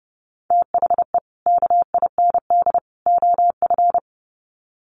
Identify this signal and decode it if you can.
Morse code, heard as a keyed tone, about 30 wpm, 700 Hz. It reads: THE KIND OF
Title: Sets of 3 Words 30wpm